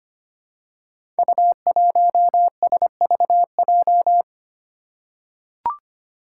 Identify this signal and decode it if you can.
Morse code, heard as a keyed tone, about 25 wpm, 700 Hz. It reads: U1SVJ
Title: Five Challenging Letter-Number Combinations 25wpm